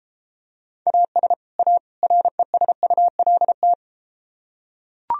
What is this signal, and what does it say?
Morse code, heard as a keyed tone, about 33 wpm, 700 Hz. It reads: AS A RESULT